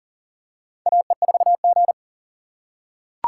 Morse code, heard as a keyed tone, about 40 wpm, 700 Hz. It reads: AE4G
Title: USA Call Signs - 4 Characters - 3x Character Spacing - 40wpm